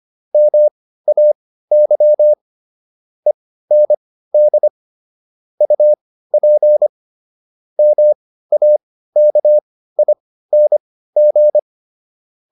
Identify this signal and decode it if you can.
Morse code, heard as a keyed tone, about 25 wpm, spaced as for 15 wpm, 600 Hz. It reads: MAY END UP MAKING